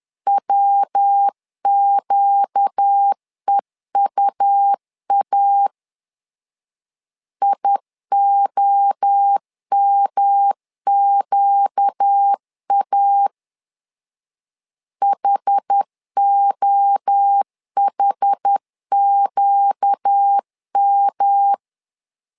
Morse code, heard as a keyed tone, about 10 wpm, 800 Hz. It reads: WQEUA IOMQA HOHQM
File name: LEZIONI / 4 Q 10wpm 2210